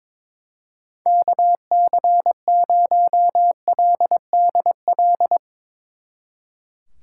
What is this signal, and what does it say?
Morse code, heard as a keyed tone, about 24 wpm, 700 Hz. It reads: KC0LDL
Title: Speed Racing - USA Call Signs 22wpm